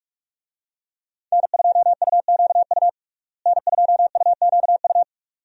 Morse code, heard as a keyed tone, about 46 wpm, 700 Hz. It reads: N2UQU N2UQU
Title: USA Call Signs - 5 Characters - Repeated 2x - 45wpm